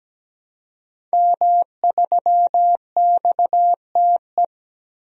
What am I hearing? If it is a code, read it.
Morse code, heard as a keyed tone, about 17 wpm, 700 Hz. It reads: M3XTE
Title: DXCC-100 Call Signs - 5 Characters - Repeated 3x - 17wpm